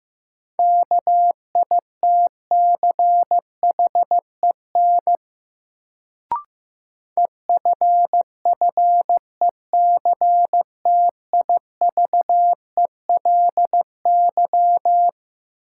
Morse code, heard as a keyed tone, about 15 wpm, 700 Hz. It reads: KITCHEN EFFECTIVELY
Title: Sentences from Top 1400 Words - New Word Drill - 15wpm